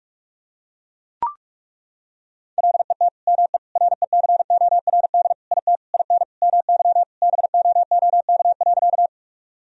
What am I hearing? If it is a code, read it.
Morse code, heard as a keyed tone, about 41 wpm, 700 Hz. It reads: LET ME RECORD IT IN MY BOOK.
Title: Sentences from Top 1000 Words - Review Part 3 - 45wpm